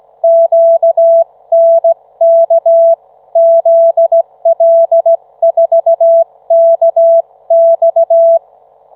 Morse code, heard as a keyed tone, about 16 wpm, 660 Hz. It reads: QNK ZL4KX